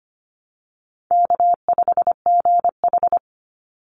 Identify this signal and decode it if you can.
Morse code, heard as a keyed tone, about 25 wpm, 700 Hz. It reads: K5GH
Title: Easy Call Signs Repeated 2x - 25wpm